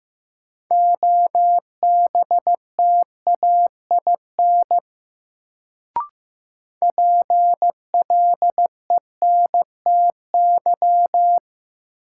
Morse code, heard as a keyed tone, about 16 wpm, 700 Hz. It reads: OBTAIN PLENTY
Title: Sentences from Top 1300 Words - New Word Drill - 15wpm